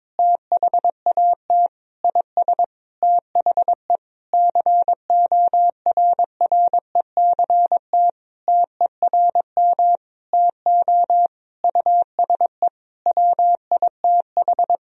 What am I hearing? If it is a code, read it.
Morse code, heard as a keyed tone, about 22 wpm, 700 Hz. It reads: THAT IS THE CORRECT TERM TO USE WITH
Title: Sentences from Top 1000 Words - Review Part 4 - 22wpm